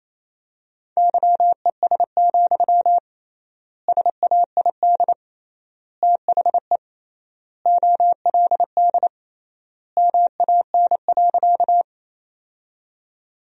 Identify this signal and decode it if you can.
Morse code, heard as a keyed tone, about 28 wpm, 700 Hz. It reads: YES, SAID THE OLD MAN.
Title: 3x Word-Spacing - Sentences from Top 500 Words - 28wpm